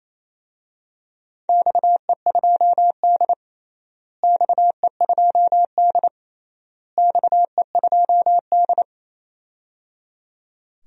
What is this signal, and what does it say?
Morse code, heard as a keyed tone, about 28 wpm, 700 Hz. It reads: XE2D XE2D XE2D
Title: DXCC-100 Call Signs - 4 Characters - Repeated 3x - 28wpm